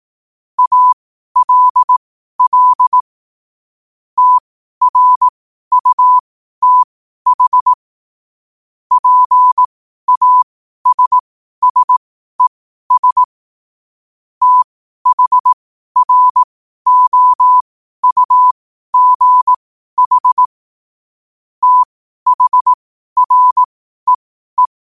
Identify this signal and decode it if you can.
Morse code, heard as a keyed tone, about 18 wpm, spaced as for 12 wpm, 1000 Hz. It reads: ALL TRUTH PASSES THROUGH THREE
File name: qotdfiles / 12wpm